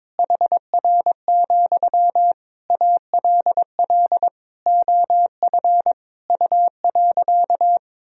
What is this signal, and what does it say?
Morse code, heard as a keyed tone, about 22 wpm, 700 Hz. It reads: HR, ALL OF U.
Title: Sentences from Top 500 Words and Abbreviations 22wpm